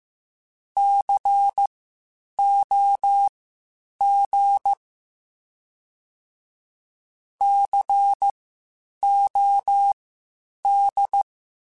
Morse code, heard as a keyed tone, about 15 wpm, spaced as for 8 wpm, 800 Hz. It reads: COG COD
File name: Spaced Alphas / Spaced Alphas mp3s / Cs sp @13